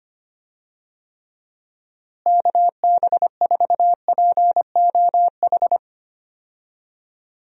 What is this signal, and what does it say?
Morse code, heard as a keyed tone, about 25 wpm, 700 Hz. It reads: KB4POH
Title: USA Call Signs - 6 Characters - Repeated 3x - 25wpm